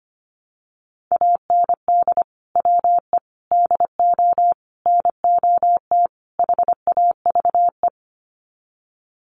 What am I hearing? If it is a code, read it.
Morse code, heard as a keyed tone, about 25 wpm, 700 Hz. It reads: AND WE DO NOT HAVE
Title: Sets of 5 Words Encore 25wpm